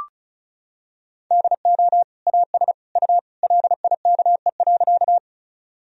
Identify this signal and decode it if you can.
Morse code, heard as a keyed tone, about 34 wpm, 700 Hz. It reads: DO AS U LIKE.
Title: Sentences from Top 100 Words and Abbreviations 35wpm